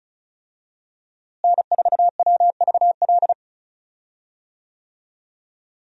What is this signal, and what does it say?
Morse code, heard as a keyed tone, about 36 wpm, 700 Hz. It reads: N4WVL
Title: USA Call Signs - 5 Characters - Repeated 2x - 35wpm